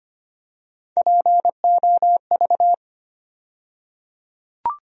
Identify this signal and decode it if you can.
Morse code, heard as a keyed tone, about 25 wpm, 700 Hz. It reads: POV
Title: USA Call Sign Postfixes - 3 Letters 25wpm